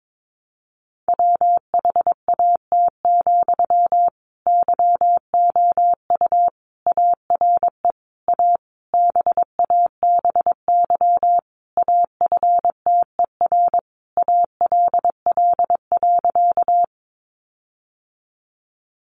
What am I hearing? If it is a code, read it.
Morse code, heard as a keyed tone, about 22 wpm, 700 Hz. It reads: WHAT, YOU ARE A BABY AFTER ALL.
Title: Sentences from Top 1300 Words 22wpm